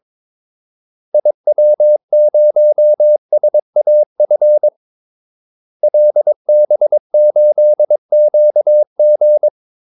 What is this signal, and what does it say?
Morse code, heard as a keyed tone, about 22 wpm, 600 Hz. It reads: IW0SAF LB8QG
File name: Callsigns-22wpm